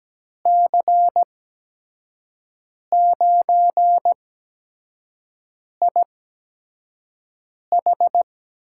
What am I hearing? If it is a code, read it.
Morse code, heard as a keyed tone, about 17 wpm, spaced as for 5 wpm, 700 Hz. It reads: C9IH
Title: USA Call Signs - 5 Characters - 8x Character Spacing - 17wpm